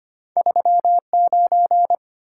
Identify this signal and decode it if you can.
Morse code, heard as a keyed tone, about 25 wpm, 690 Hz. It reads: 39
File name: Two Letter-Number Combinations - 25wpm